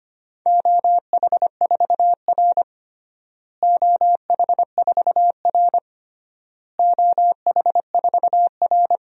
Morse code, heard as a keyed tone, about 25 wpm, 700 Hz. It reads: OH4R OH4R OH4R
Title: DXCC-100 Call Signs - 4 Characters - Repeated 3x - 25wpm